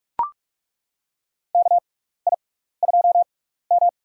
Morse code, heard as a keyed tone, about 42 wpm, spaced as for 15 wpm, 700 Hz. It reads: KI2M
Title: USA Call Signs - 4 Characters - 6x Character Spacing - 45wpm